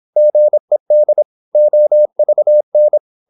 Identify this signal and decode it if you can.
Morse code, heard as a keyed tone, about 26 wpm, 600 Hz. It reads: GED OVN
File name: Korte ord DK 26wpm